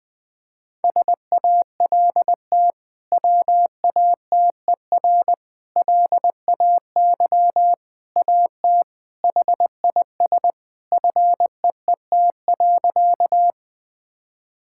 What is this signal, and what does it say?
Morse code, heard as a keyed tone, about 21 wpm, 700 Hz. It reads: SALT WATER LAY AT HIS FEET.
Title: Sentences from Top 1000 Words - Review Part 5 - 20wpm